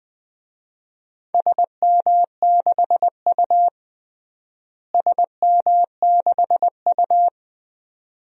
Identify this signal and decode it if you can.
Morse code, heard as a keyed tone, about 21 wpm, 700 Hz. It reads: SM6U SM6U
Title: DXCC-100 Call Signs - 4 Characters - Repeated 2x - 20wpm